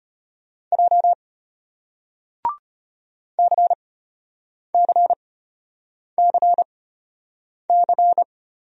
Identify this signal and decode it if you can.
Morse code, heard as a keyed tone, about 33 wpm, spaced as for 8 wpm, 700 Hz. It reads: J CCCC